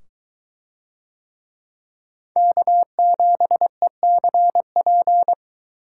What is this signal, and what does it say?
Morse code, heard as a keyed tone, about 23 wpm, 700 Hz. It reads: K7ECP